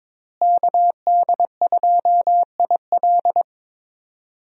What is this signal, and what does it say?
Morse code, heard as a keyed tone, about 22 wpm, 700 Hz. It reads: KD2IL
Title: USA Call Signs - 5 Characters - Repeated 2x - 22wpm